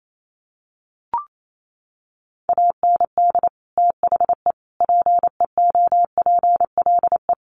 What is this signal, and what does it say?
Morse code, heard as a keyed tone, about 28 wpm, 700 Hz. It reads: AND THE PEOPLE
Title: Sentences from Top 1500 Words 28wpm